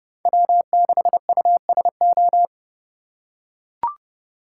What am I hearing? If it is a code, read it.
Morse code, heard as a keyed tone, about 30 wpm, 700 Hz. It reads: W6USO
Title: USA Call Signs - 5 Characters - 2x Character Spacing - 30wpm